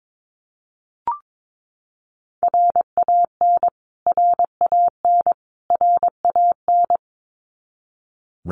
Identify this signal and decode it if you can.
Morse code, heard as a keyed tone, about 22 wpm, 700 Hz. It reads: RAN RAN RAN
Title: Top 500 Words Repeated 22wpm